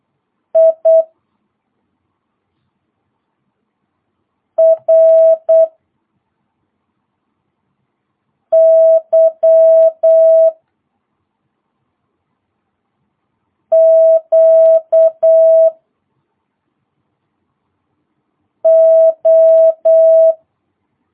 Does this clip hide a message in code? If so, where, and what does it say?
Yes: Morse code from 0.6 to 20.3 s, heard as a keyed tone, about 8 wpm, 650 Hz. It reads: I R Y Q O